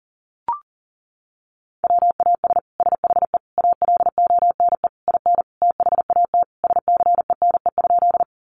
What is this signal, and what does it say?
Morse code, heard as a keyed tone, about 40 wpm, 700 Hz. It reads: WAS SHE ALONE IN THAT SCENE?
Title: Sentences from Top 1100 Words 40wpm